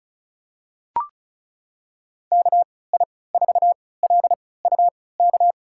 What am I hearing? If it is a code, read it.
Morse code, heard as a keyed tone, about 34 wpm, spaced as for 20 wpm, 700 Hz. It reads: KI4LUK
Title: USA Call Signs - 6 Characters - 3x Character Spacing - 35wpm